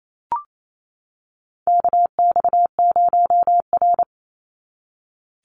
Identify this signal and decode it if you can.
Morse code, heard as a keyed tone, about 28 wpm, 700 Hz. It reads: KX0R
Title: NA SOTA Chaser Callsigns 28wpm